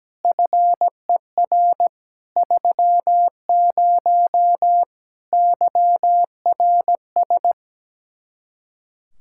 Morse code, heard as a keyed tone, about 17 wpm, 700 Hz. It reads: FER 30 YRS
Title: K1USN SST QSO Practice 17wpm